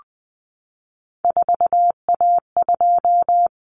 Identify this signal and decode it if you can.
Morse code, heard as a keyed tone, about 20 wpm, 700 Hz. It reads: 4A2
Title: Common Cut Numbers - 3-Digit 20wpm